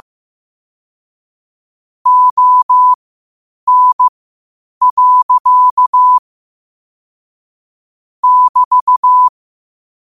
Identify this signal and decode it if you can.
Morse code, heard as a keyed tone, about 15 wpm, spaced as for 8 wpm, 1000 Hz. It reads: ON. <BT>
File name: qotdfiles / 8wpm